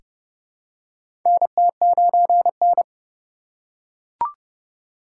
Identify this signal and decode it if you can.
Morse code, heard as a keyed tone, about 31 wpm, 700 Hz. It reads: NT9N
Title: USA Call Signs - 4 Characters - 6x Character Spacing - 30wpm